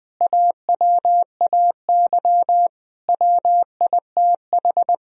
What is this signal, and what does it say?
Morse code, heard as a keyed tone, about 20 wpm, 700 Hz. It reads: AWAY WITH
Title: Sentences from Top 1000 Words - Review Part 1 - 20wpm